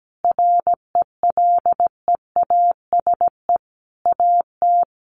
Morse code, heard as a keyed tone, about 17 wpm, 700 Hz. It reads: RELEASE AT
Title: Sentences from Top 1100 Words 17wpm